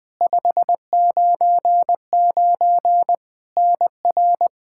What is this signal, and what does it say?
Morse code, heard as a keyed tone, about 21 wpm, 700 Hz. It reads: 599 NR
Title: Practice QSO - Exchange 1 - Beginning - 20wpm